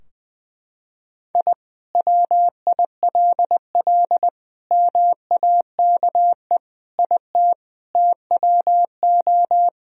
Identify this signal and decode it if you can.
Morse code, heard as a keyed tone, about 20 wpm, 700 Hz. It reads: I WILL MAKE IT TWO
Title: Sentences from Top 1000 Words - Review Part 2 - 20wpm